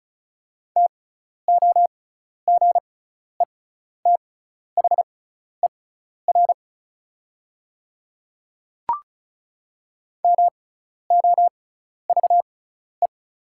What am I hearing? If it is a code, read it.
Morse code, heard as a keyed tone, about 36 wpm, spaced as for 12 wpm, 700 Hz. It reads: TOGETHER MOVE
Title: 6x Character Spacing - Top 500 Words - 35wpm